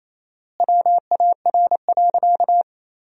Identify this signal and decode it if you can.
Morse code, heard as a keyed tone, about 28 wpm, 700 Hz. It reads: WAR.
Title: 4x Word-Spacing - Sentences from Top 400 Words - 28wpm